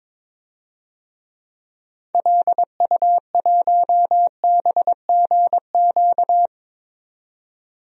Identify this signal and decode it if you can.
Morse code, heard as a keyed tone, about 22 wpm, 700 Hz. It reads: LU1BGQ
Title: DXCC-100 Call Signs - 6 Characters - 22wpm